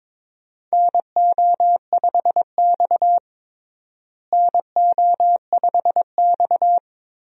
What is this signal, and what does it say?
Morse code, heard as a keyed tone, about 22 wpm, 700 Hz. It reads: NO5X NO5X
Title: USA Call Signs - 4 Characters - Repeated 2x - 22wpm